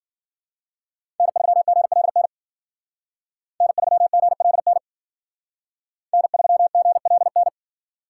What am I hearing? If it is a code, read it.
Morse code, heard as a keyed tone, about 52 wpm, 700 Hz. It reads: N3GLN N3GLN N3GLN